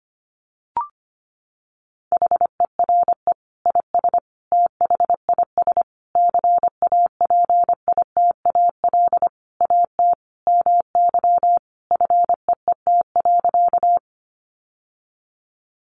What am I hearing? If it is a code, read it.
Morse code, heard as a keyed tone, about 25 wpm, 700 Hz. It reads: HERE IS THIS CAPITAL AT MY FEET.
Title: Sentences from Top 1000 Words 25wpm